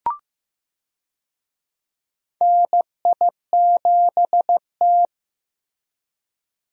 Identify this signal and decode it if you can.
Morse code, heard as a keyed tone, about 16 wpm, 700 Hz. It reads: NI7T
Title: USA Call Signs - 4 Characters - Repeated 3x - 15wpm